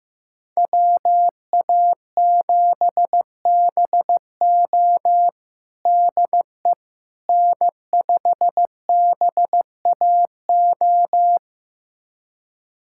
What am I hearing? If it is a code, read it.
Morse code, heard as a keyed tone, about 15 wpm, 700 Hz. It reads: WA7BO DE N5BAO
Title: USA Callsign DE Callsign - 5 Characters - 15wpm